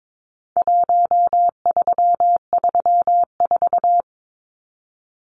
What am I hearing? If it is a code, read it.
Morse code, heard as a keyed tone, about 22 wpm, 700 Hz. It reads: 1334
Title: Numbers - Contest Style - 4-Digit 22wpm